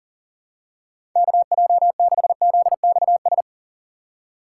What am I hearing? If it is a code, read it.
Morse code, heard as a keyed tone, about 40 wpm, 700 Hz. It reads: KJ6ZXS
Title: USA Call Signs - 6x Character Spacing - 40wpm